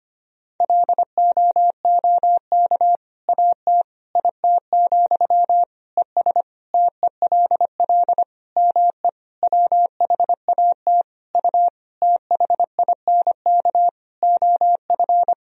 Morse code, heard as a keyed tone, about 25 wpm, 700 Hz. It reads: LOOK AT IT, ES TELL ME WHAT U THINK OF
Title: Sentences from Top 200 Words and Abbreviations 25wpm